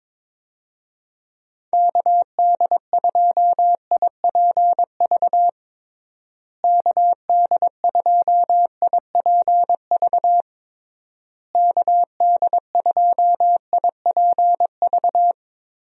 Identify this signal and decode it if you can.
Morse code, heard as a keyed tone, about 22 wpm, 700 Hz. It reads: KD2IPV KD2IPV KD2IPV
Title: USA Call Signs - 6 Characters - Repeated 3x - 22wpm